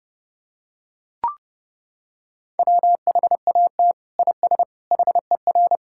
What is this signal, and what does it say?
Morse code, heard as a keyed tone, about 32 wpm, 700 Hz. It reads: WHAT IS HER
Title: Sentences from Top 1000 Words - Review Part 6 - 30wpm